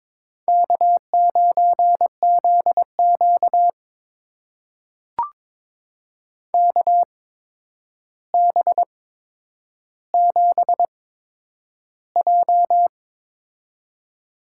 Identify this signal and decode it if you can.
Morse code, heard as a keyed tone, about 22 wpm, spaced as for 9 wpm, 700 Hz. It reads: K9ZQ KB7J
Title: USA Call Signs - 8x Character Spacing - 22wpm